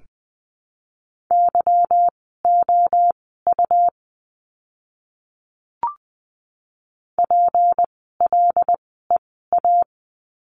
Morse code, heard as a keyed tone, about 20 wpm, spaced as for 14 wpm, 700 Hz. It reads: YOU PLEA
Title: 153 Lesson 25f - Letter p - Words 20wpm